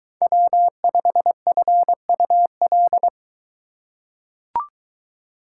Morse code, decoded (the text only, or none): W5FUL